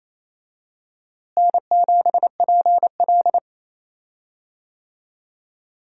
Morse code, heard as a keyed tone, about 28 wpm, 700 Hz. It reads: N7PL
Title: USA Call Signs - 4 Characters - Repeated 3x - 28wpm